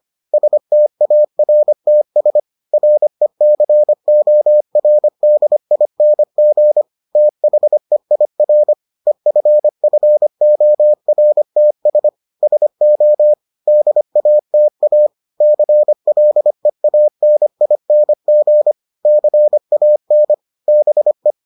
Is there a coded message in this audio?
Yes: Morse code from 0.3 to 21.3 s, heard as a keyed tone, about 25 wpm, 600 Hz. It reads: STARTS RECORDING THEIR EFFORTS SO DATA CLEANING CAN BE